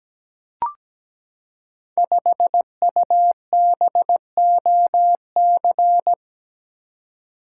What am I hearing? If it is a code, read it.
Morse code, heard as a keyed tone, about 17 wpm, 700 Hz. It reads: 5UBOC
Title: Code Groups - 5 Letter:Numbers - 17wpm